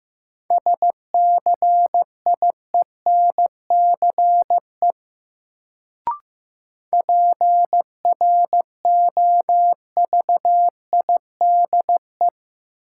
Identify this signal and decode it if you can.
Morse code, heard as a keyed tone, about 15 wpm, 700 Hz. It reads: SCIENCE PROVIDE
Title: Sentences from Top 1000 Words - Review Part 2 - New Word Drill - 15wpm